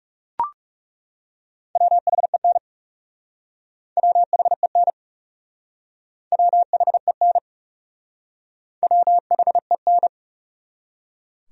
Morse code, heard as a keyed tone, about 37 wpm, 700 Hz. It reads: WHEN WHEN WHEN WHEN
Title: Speed-Racing Top 300 Words 30wpm